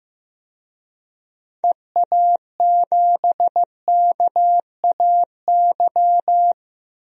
Morse code, heard as a keyed tone, about 15 wpm, 700 Hz. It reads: EA7KAY
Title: DXCC-100 Call Signs - 6 Characters - 15wpm